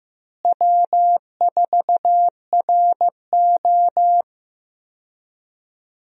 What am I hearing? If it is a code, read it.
Morse code, heard as a keyed tone, about 15 wpm, 700 Hz. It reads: W4RO
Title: USA Call Signs - 4 Characters - Repeated 2x - 15wpm